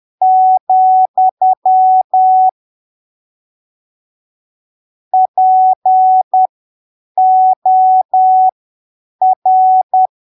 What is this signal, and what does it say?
Morse code, heard as a keyed tone, about 10 wpm, spaced as for 7 wpm, 750 Hz. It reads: , POR
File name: arquivos / texto / texto 20 10ppm